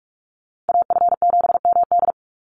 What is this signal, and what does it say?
Morse code, heard as a keyed tone, about 44 wpm, 700 Hz. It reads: AF7MD
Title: NA SOTA Chaser Callsigns 45wpm